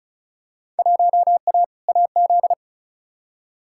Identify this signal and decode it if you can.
Morse code, heard as a keyed tone, about 36 wpm, 700 Hz. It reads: 1A AZ
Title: Field Day Exchanges - 35wpm